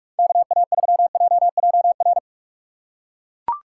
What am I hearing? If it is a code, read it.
Morse code, heard as a keyed tone, about 42 wpm, 700 Hz. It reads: KA3JJR
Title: USA Call Signs - 4x Character Spacing - 45wpm